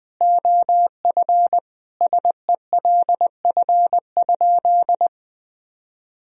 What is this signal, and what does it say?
Morse code, heard as a keyed tone, about 20 wpm, 700 Hz. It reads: OF SELF?
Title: Sentences from Top 1000 Words - Review Part 1 - 20wpm